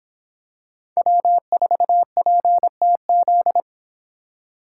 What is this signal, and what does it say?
Morse code, heard as a keyed tone, about 22 wpm, 700 Hz. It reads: W4PTZ